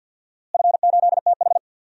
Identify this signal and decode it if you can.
Morse code, heard as a keyed tone, about 52 wpm, 700 Hz. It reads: F8TH
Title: DXCC-100 Call Signs - 4 Characters - Repeated 2x - 50wpm